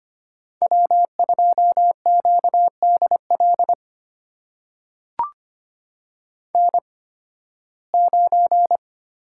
Morse code, heard as a keyed tone, about 25 wpm, spaced as for 16 wpm, 700 Hz. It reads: W2QDL N9
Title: USA Call Signs - 5 Characters - 8x Character Spacing - 25wpm